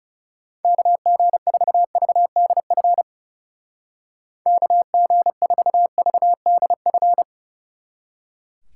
Morse code, heard as a keyed tone, about 38 wpm, 700 Hz. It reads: KG4VDF KG4VDF